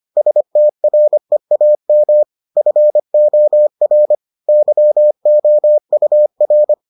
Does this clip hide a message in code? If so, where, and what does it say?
Yes: Morse code from 0.2 to 6.7 s, heard as a keyed tone, about 25 wpm, 600 Hz. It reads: STREAM FOR YOUR